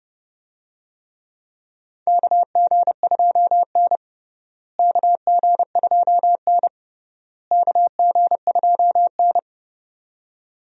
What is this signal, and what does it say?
Morse code, heard as a keyed tone, about 30 wpm, 700 Hz. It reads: KG2N KG2N KG2N
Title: USA Call Signs - 4 Characters - Repeated 3x - 30wpm